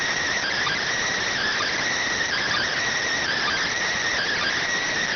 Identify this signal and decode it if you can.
Emergency Alert System (SAME) tones, heard as harsh buzzing bursts. It is heard from 0.4 to 4.6 s.
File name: findings / SAGITTAL 43mm 4mm.gen-06 rx